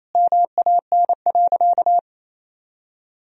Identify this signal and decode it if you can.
Morse code, heard as a keyed tone, about 28 wpm, 700 Hz. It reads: MAN.
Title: Sentences from Top 500 Words and Abbreviations 28wpm